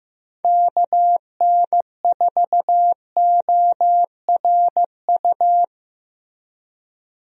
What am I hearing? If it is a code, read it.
Morse code, heard as a keyed tone, about 15 wpm, 700 Hz. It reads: KN4ORU
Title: USA Call Signs - 6 Characters - 15wpm